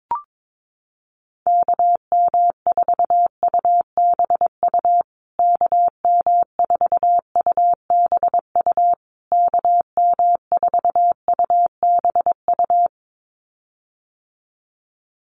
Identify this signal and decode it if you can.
Morse code, heard as a keyed tone, about 22 wpm, 700 Hz. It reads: KM4UBU KM4UBU KM4UBU
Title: USA Call Signs Repeated 3x 22wpm